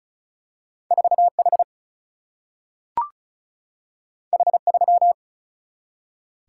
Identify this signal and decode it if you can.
Morse code, heard as a keyed tone, about 36 wpm, 700 Hz. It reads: 4 H H 3
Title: Single-Letter Single-Number 35wpm